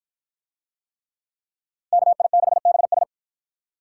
Morse code, heard as a keyed tone, about 50 wpm, 700 Hz. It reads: KI6BS